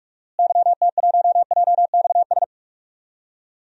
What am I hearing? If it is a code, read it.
Morse code, heard as a keyed tone, about 43 wpm, 700 Hz. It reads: YT1JXS
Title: DXCC-100 Call Signs - 6 Characters - Repeated 2x - 45wpm